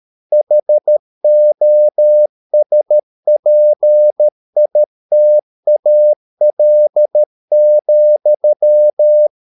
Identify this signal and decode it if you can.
Morse code, heard as a keyed tone, about 13 wpm, 600 Hz. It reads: HOSPITAL,